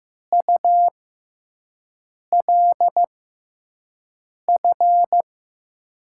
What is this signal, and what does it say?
Morse code, heard as a keyed tone, about 15 wpm, spaced as for 5 wpm, 700 Hz. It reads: ULF
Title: USA Call Signs - 6 Characters - 6x Character Spacing - 15wpm